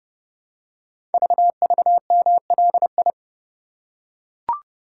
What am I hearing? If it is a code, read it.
Morse code, heard as a keyed tone, about 30 wpm, 700 Hz. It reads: VVMLI
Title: Code Groups - 5 Letters - 30wpm